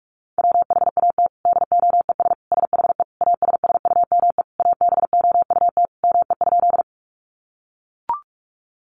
Morse code, heard as a keyed tone, about 44 wpm, 700 Hz. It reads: WHAT DOES SHE ASSUME ABOUT ME?
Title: Sentences from Top 1500 Words 45wpm